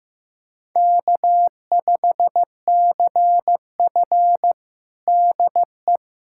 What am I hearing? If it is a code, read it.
Morse code, heard as a keyed tone, about 15 wpm, 700 Hz. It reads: K5CF DE
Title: USA Callsign DE Callsign - 4 Characters - 15wpm